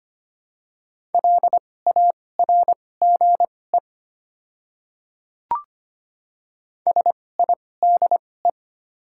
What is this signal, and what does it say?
Morse code, heard as a keyed tone, about 25 wpm, spaced as for 18 wpm, 700 Hz. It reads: LARGE SIDE
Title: 2x Character Spacing - Top 200 Words - 25wpm